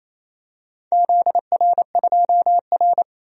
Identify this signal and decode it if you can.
Morse code, heard as a keyed tone, about 28 wpm, 700 Hz. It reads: ZR2R
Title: DXCC-100 Call Signs - 4 Characters - 28wpm